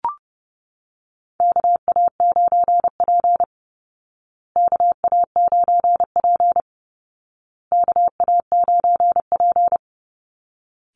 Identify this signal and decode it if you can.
Morse code, heard as a keyed tone, about 30 wpm, 700 Hz. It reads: KA9P KA9P KA9P
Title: Call Signs Repeated 2x - 30wpm